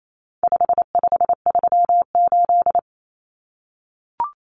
Morse code, heard as a keyed tone, about 28 wpm, 700 Hz. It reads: 5538
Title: Common Cut Numbers - 4-Digit 28wpm